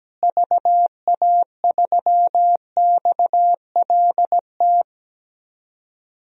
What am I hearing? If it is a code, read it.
Morse code, heard as a keyed tone, about 17 wpm, 700 Hz. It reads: VA3XLT
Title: DXCC-100 Call Signs - 6 Characters - Repeated 3x - 17wpm